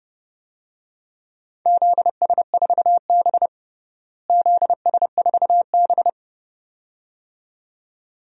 Morse code, heard as a keyed tone, about 31 wpm, 700 Hz. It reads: ZS4B ZS4B
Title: DXCC-100 Call Signs - 4 Characters - Repeated 2x - 30wpm